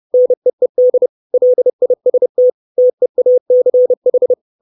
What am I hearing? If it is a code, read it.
Morse code, heard as a keyed tone, about 30 wpm, 500 Hz. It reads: NEED LIST TEACH
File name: 632 Words 30WPM